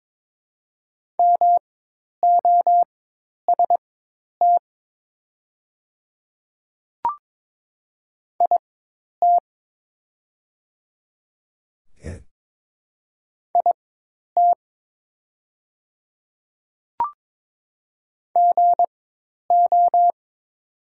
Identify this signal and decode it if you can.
Morse code, heard as a keyed tone, about 22 wpm, spaced as for 10 wpm, 700 Hz. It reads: MOST IT IT GO
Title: 4x Character Spacing - Top 100 Words - 22wpm